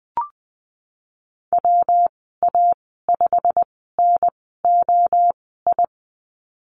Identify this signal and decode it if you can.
Morse code, heard as a keyed tone, about 19 wpm, spaced as for 14 wpm, 700 Hz. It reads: WA5NOI
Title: 238 Lesson 38g - Letter q - Call Signs 20wpm